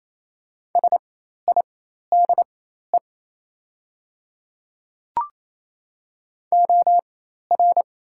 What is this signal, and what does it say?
Morse code, heard as a keyed tone, about 28 wpm, spaced as for 13 wpm, 700 Hz. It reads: SIDE OR
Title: 4x Character Spacing - Top 500 Words - 28wpm